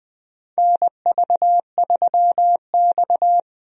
Morse code, heard as a keyed tone, about 20 wpm, 700 Hz. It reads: NV3X
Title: USA Call Signs - 4 Characters - 20wpm